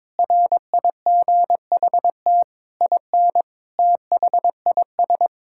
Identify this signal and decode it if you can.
Morse code, heard as a keyed tone, about 22 wpm, 700 Hz. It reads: RIGHT IN THIS
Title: Sentences from Top 1000 Words - Review Part 5 - 22wpm